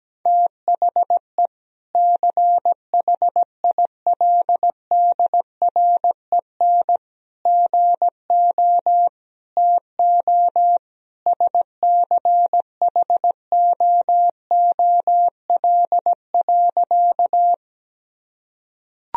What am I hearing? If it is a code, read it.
Morse code, heard as a keyed tone, about 17 wpm, 700 Hz. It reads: THE CHILDREN GO TO SCHOOL.
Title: Sentences from Top 1000 Words - Review Part 1 - 17wpm